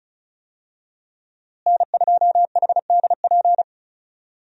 Morse code, heard as a keyed tone, about 34 wpm, 700 Hz. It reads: N2HDP
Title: USA Call Signs - 5 Characters - Repeated 2x - 35wpm